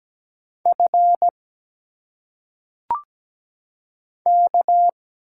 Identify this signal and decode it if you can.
Morse code, heard as a keyed tone, about 17 wpm, 700 Hz. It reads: F K
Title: Single Letters - Phonetic Pronunciation 17wpm